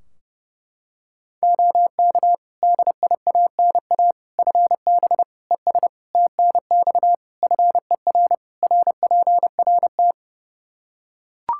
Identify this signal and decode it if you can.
Morse code, heard as a keyed tone, about 30 wpm, 700 Hz. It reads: OK DIANA FB ES TNX FER RPRT
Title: Practice QSO - Exchange 2 - Info - 30wpm